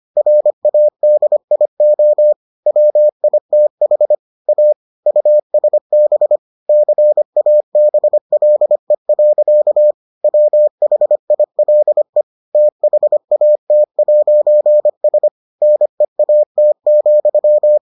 Morse code, heard as a keyed tone, about 25 wpm, 600 Hz. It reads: RADIO WITH A USB CABLE. WHILE THAT'S NEAT,